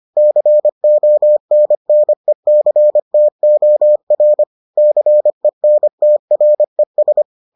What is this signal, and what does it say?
Morse code, heard as a keyed tone, about 25 wpm, 600 Hz. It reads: CONNECTOR CENTRES